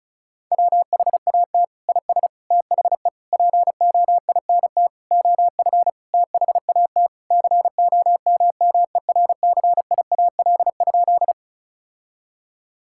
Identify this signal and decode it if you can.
Morse code, heard as a keyed tone, about 36 wpm, 700 Hz. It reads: WHAT IS THE POINT OF THAT COMMERCIAL?
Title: Sentences from Top 1000 Words - Review Part 5 - 35wpm